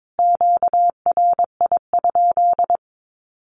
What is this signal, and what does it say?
Morse code, heard as a keyed tone, about 22 wpm, 700 Hz. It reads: QRI?
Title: Q-Codes Repeated 3x 22wpm